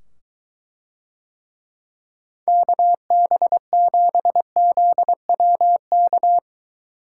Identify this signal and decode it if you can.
Morse code, heard as a keyed tone, about 23 wpm, 700 Hz. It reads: KB7ZWK